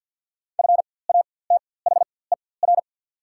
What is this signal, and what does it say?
Morse code, heard as a keyed tone, about 50 wpm, spaced as for 23 wpm, 700 Hz. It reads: FATHER
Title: 4x Character Spacing - Top 500 Words - 50wpm